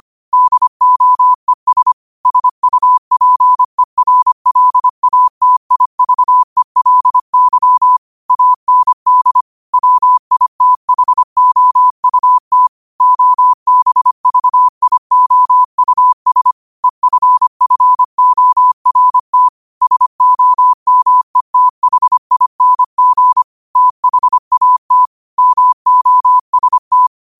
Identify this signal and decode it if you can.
Morse code, heard as a keyed tone, about 25 wpm, 1000 Hz. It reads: DOES SUPERLATIVELY AND WITHOUT OBVIOUS EFFORT SOMETHING THAT MOST
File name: qotdfiles / 25wpm